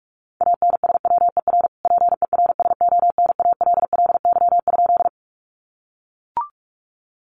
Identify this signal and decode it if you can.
Morse code, heard as a keyed tone, about 44 wpm, 700 Hz. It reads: ANSWER PERSONALLY?
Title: Sentences from Top 1400 Words 45wpm